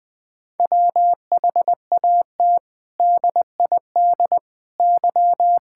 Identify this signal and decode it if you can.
Morse code, heard as a keyed tone, about 20 wpm, 700 Hz. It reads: WHAT DID Y
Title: Sentences from Top 1000 Words - Review Part 6 - 20wpm